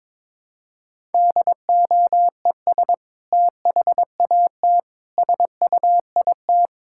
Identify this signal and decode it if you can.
Morse code, heard as a keyed tone, about 22 wpm, 700 Hz. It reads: DOES THAT SUIT
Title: Sentences from Top 1600 Words 22wpm